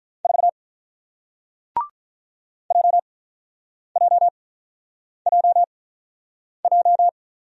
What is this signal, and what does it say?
Morse code, heard as a keyed tone, about 46 wpm, spaced as for 9 wpm, 700 Hz. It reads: 4 JJJJ